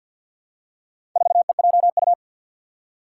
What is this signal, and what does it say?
Morse code, heard as a keyed tone, about 50 wpm, 700 Hz. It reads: 4EJU
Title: Four Challenging Letter-Number Combinations 50wpm